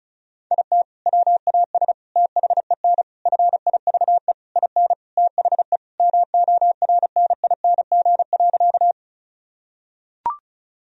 Morse code, heard as a keyed tone, about 36 wpm, 700 Hz. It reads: IT WAS THEN FIVE IN THE MORNING.
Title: Sentences from Top 1000 Words - Review Part 2 - 35wpm